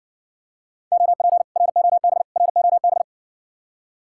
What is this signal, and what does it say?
Morse code, heard as a keyed tone, about 60 wpm, 700 Hz. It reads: OP ROB ROB